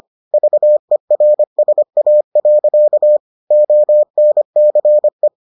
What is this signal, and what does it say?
Morse code, heard as a keyed tone, about 25 wpm, 600 Hz. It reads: VERSA. ONCE